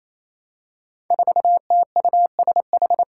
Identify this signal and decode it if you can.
Morse code, heard as a keyed tone, about 28 wpm, 700 Hz. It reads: 4TUSH
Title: Five Challenging Letter-Number Combinations 28wpm